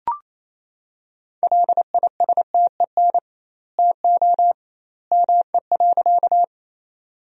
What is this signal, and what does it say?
Morse code, heard as a keyed tone, about 28 wpm, 700 Hz. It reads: LISTEN TO ME.
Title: 2x Word-Spacing - Sentences from Top 400 Words - 28wpm